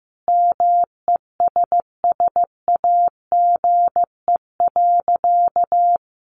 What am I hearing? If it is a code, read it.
Morse code, heard as a keyed tone, about 15 wpm, 700 Hz. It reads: MESSAGE.
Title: Sentences from Top 1100 Words 15wpm